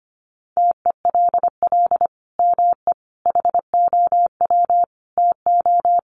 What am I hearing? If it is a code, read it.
Morse code, heard as a keyed tone, about 25 wpm, 700 Hz. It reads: TELL ME HOW TO
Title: Sentences from Top 1500 Words 25wpm